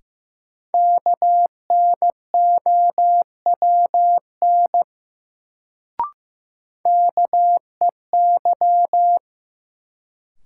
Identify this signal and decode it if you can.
Morse code, heard as a keyed tone, about 15 wpm, 700 Hz. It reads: KNOWN KEY
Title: Sentences from Top 600 Words - New Word Drill - 15wpm